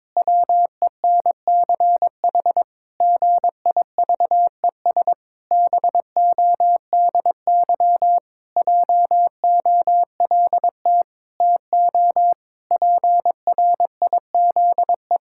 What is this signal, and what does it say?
Morse code, heard as a keyed tone, about 22 wpm, 700 Hz. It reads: WENCH GIVES BODY JOLT TO PRIZE
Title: Pangram Challenge v2 - 22wpm